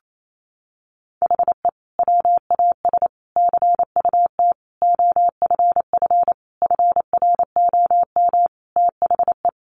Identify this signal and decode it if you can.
Morse code, heard as a keyed tone, about 28 wpm, 700 Hz. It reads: HE WAS CUT OFF FROM THE
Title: Sentences from Top 1100 Words 28wpm